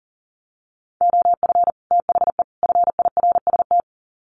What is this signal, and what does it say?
Morse code, heard as a keyed tone, about 40 wpm, 700 Hz. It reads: OF THE FIRST
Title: Sets of 3 Words 40wpm